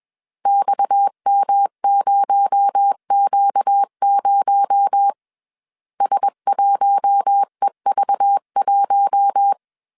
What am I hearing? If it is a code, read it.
Morse code, heard as a keyed tone, about 21 wpm, 800 Hz. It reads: XM0Q0 S1EV1
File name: CW / DET / 1 6 1500C S B 20wpm 2048